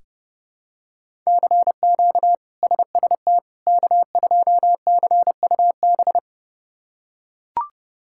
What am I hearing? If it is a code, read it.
Morse code, heard as a keyed tone, about 30 wpm, 700 Hz. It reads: CQ SST K2CUB
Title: K1USN SST QSO Practice - 30wpm